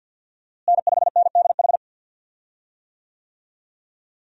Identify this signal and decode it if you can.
Morse code, heard as a keyed tone, about 51 wpm, 700 Hz. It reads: N5NDH
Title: USA Call Signs - 5 Characters - Repeated 2x - 50wpm